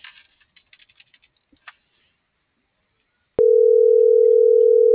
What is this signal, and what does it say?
Ringback tone (North America), 440 + 480 Hz.